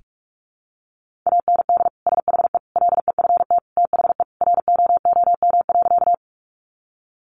Morse code, heard as a keyed tone, about 44 wpm, 700 Hz. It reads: AND SHE LEFT THE ROOM.
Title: Sentences from Top 300 Words 45wpm